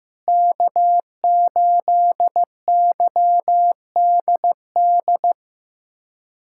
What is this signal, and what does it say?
Morse code, heard as a keyed tone, about 15 wpm, 700 Hz. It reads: K8YDD
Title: USA Callsign DE Callsign - 5 Characters - 15wpm